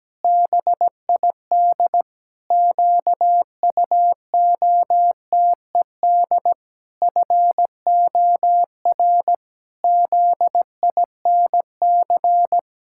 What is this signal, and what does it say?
Morse code, heard as a keyed tone, about 17 wpm, 700 Hz. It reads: BID QUOTED FOR ZINC
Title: Pangram Challenge v2 - 17wpm